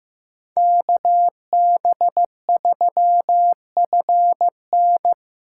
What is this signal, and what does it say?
Morse code, heard as a keyed tone, about 18 wpm, 700 Hz. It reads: KB3FN
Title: Speed Racing - USA 2x2 Call Signs 15wpm